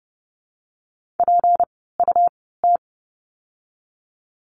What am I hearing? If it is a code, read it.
Morse code, heard as a keyed tone, about 30 wpm, spaced as for 17 wpm, 700 Hz. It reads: PUT
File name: course / 30:10wpm / 237 Lesson 38f - Letter q - Words 30wpm